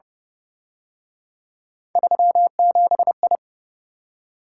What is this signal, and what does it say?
Morse code, heard as a keyed tone, about 29 wpm, 700 Hz. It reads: 37I
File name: Speed-Racing - Three Letter-Number Combinations - 25wpm